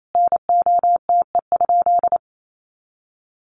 Morse code, heard as a keyed tone, about 28 wpm, 700 Hz. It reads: NOTE?
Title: Sentences from Top 500 Words 28wpm